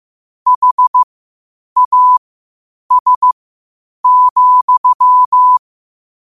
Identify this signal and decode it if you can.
Morse code, heard as a keyed tone, about 15 wpm, spaced as for 8 wpm, 1000 Hz. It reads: HAS,
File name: qotdfiles / 8wpm